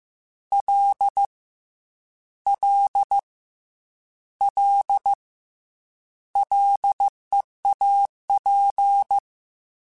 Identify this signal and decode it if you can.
Morse code, heard as a keyed tone, about 15 wpm, 800 Hz. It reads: L L L LEAP
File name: Combinations / Combinations-mp3s / Combinations@13